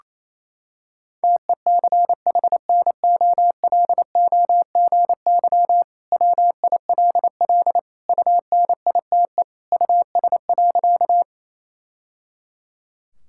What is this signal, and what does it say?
Morse code, heard as a keyed tone, about 28 wpm, 700 Hz. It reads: TECHNOLOGY WILL UNITE US.
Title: Sentences from Top 1000 Words - Review Part 5 - 28wpm